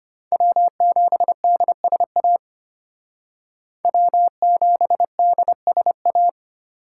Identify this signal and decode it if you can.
Morse code, heard as a keyed tone, about 29 wpm, 700 Hz. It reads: W7DSA W7DSA
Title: Speed Racing - USA 1x3 Call Signs 25wpm